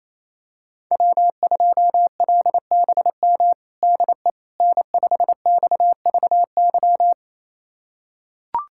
Morse code, heard as a keyed tone, about 28 wpm, 700 Hz. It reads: W2LBM DE N5XVY
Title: USA Callsign DE Callsign 28wpm